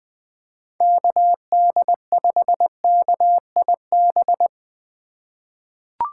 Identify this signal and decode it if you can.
Morse code, heard as a keyed tone, about 21 wpm, 700 Hz. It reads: KD5KIB
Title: USA Call Signs - 6 Characters - 3x Character Spacing - 20wpm